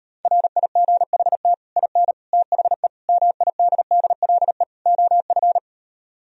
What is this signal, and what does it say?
Morse code, heard as a keyed tone, about 38 wpm, 700 Hz. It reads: RIGHT IN THE MIDDLE OF